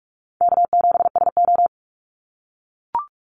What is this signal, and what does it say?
Morse code, heard as a keyed tone, about 44 wpm, 700 Hz. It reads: K7SO
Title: NA SOTA Activator Callsigns 45wpm